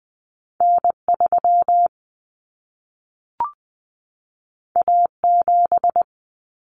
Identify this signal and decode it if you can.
Morse code, heard as a keyed tone, about 20 wpm, 700 Hz. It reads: N3 A7
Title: Common Cut Numbers - 2-Digit 20wpm